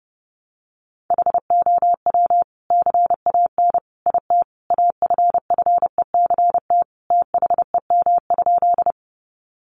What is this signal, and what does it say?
Morse code, heard as a keyed tone, about 30 wpm, 700 Hz. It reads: HOW CAN IT AFFECT THEM?
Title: Sentences from Top 900 Words 30wpm